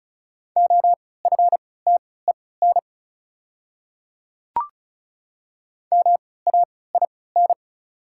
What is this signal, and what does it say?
Morse code, heard as a keyed tone, about 36 wpm, spaced as for 20 wpm, 690 Hz. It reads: OFTEN MAIN
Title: 3x Character Spacing - Top 500 Words - 35wpm